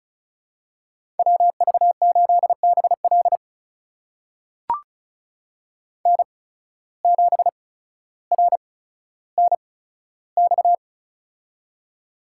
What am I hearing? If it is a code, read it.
Morse code, heard as a keyed tone, about 36 wpm, spaced as for 15 wpm, 700 Hz. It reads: WV8BL N7RNX
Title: USA Call Signs - 5 Characters - 8x Character Spacing - 35wpm